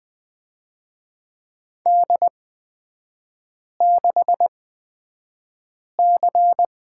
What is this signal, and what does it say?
Morse code, heard as a keyed tone, about 20 wpm, spaced as for 5 wpm, 700 Hz. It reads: D6C
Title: 250 Lesson 40e - BK - Three Characters 20wpm